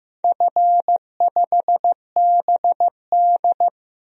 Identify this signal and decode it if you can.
Morse code, heard as a keyed tone, about 15 wpm, 700 Hz. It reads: F5BD
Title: DXCC-100 Call Signs - 4 Characters - 15wpm